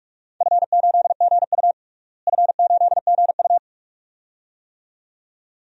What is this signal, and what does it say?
Morse code, heard as a keyed tone, about 46 wpm, 700 Hz. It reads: F8GU F8GU
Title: DXCC-100 Call Signs - 4 Characters - Repeated 2x - 45wpm